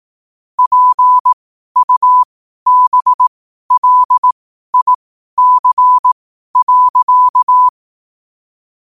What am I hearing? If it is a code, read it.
Morse code, heard as a keyed tone, about 18 wpm, spaced as for 12 wpm, 1000 Hz. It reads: PUBLIC.
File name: qotdfiles / 12wpm